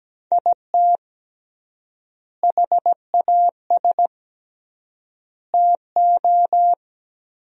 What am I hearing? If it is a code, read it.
Morse code, heard as a keyed tone, about 17 wpm, 700 Hz. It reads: IT HAS TO
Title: 3x Word-Spacing - Sentences from Top 500 Words - 17wpm